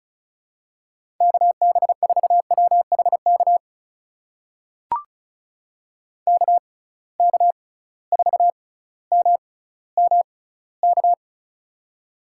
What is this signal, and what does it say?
Morse code, heard as a keyed tone, about 36 wpm, spaced as for 18 wpm, 690 Hz. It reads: KB4WHK KK4MMK
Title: USA Call Signs - 6 Characters - 6x Character Spacing - 35wpm